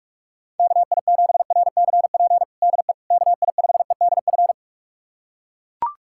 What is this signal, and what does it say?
Morse code, heard as a keyed tone, about 41 wpm, 700 Hz. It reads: KI7RCP DE KI5EDF
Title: USA Callsign DE Callsign 45wpm